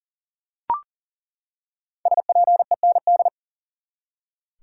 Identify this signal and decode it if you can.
Morse code, heard as a keyed tone, about 37 wpm, 700 Hz. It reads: SPEND
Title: Five Letter Words - 40wpm